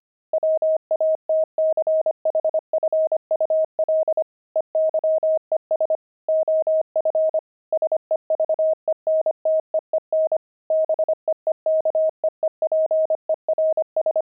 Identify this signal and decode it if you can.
Morse code, heard as a keyed tone, about 25 wpm, 630 Hz. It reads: WATCHFUL EYES OF SEVENTEEN BEEKEEPERS
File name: fs-25-cs-25